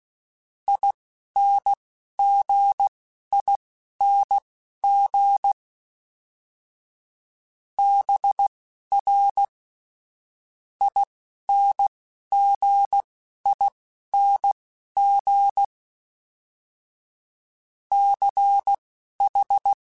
Morse code, heard as a keyed tone, about 16 wpm, spaced as for 11 wpm, 800 Hz. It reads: INGING BR INGING CH
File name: Special INGs / Special INGs-mp3s / Dble Ings sp @10